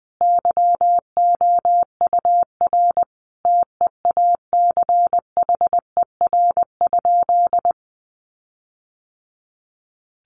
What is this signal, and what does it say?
Morse code, heard as a keyed tone, about 19 wpm, 700 Hz. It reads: YOUR TEACHER?
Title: Sentences from Top 1300 Words 20wpm